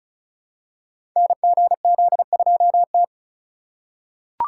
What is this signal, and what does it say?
Morse code, heard as a keyed tone, about 35 wpm, 700 Hz. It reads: NGZ2T
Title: Code Groups - 5 Letter:Numbers - 35wpm